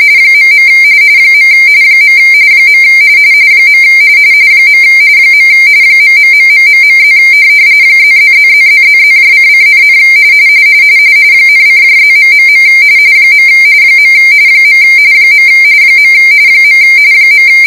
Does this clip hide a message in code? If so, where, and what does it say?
Yes: RTTY from 0.3 to 17.5 s, about 45.45 baud. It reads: .. . . . ......+ +++++++::,, ,:+... . . .